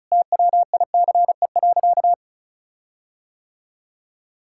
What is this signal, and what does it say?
Morse code, heard as a keyed tone, about 34 wpm, 700 Hz. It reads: TWICE.
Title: Sentences from Top 1600 Words 35wpm